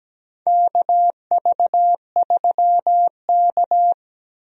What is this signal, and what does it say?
Morse code, heard as a keyed tone, about 17 wpm, 700 Hz. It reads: KV3K
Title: USA Call Signs - 4 Characters - Repeated 2x - 17wpm